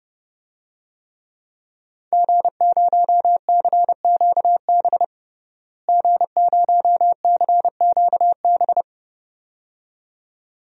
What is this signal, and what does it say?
Morse code, heard as a keyed tone, about 30 wpm, 700 Hz. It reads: G0CQB G0CQB
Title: DXCC-100 Call Signs - 5 Characters - Repeated 2x - 30wpm